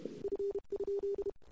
Morse code, heard as a keyed tone, about 31 wpm, 380 Hz. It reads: F?
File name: CW / 2008-ss-cw-k5tr-op-n5rz -22:00